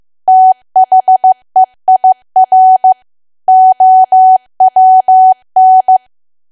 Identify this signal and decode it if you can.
Morse code, heard as a keyed tone, about 15 wpm, 750 Hz. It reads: THEIR OWN
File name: cwa / W1AW-Level2 / 15WPM